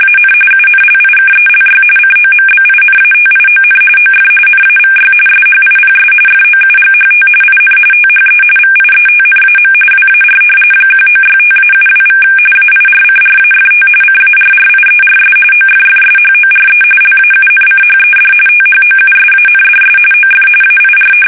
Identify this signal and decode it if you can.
RTTY, about 45.45 baud. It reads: AN UNLIKELY COMBINATION OF NAVY SHIPS FOUGHT IN THE VIETNAM WAR 1965-72= AIRCRAFT CARRIERS OFFSHORE LAUNCHED THOUSANDS OF AIR
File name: fsk850 / usnavy5